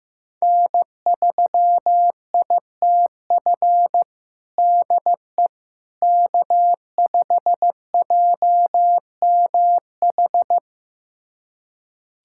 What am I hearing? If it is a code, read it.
Morse code, heard as a keyed tone, about 15 wpm, 700 Hz. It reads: N3ITF DE K5JMH
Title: USA Callsign DE Callsign - 5 Characters - 15wpm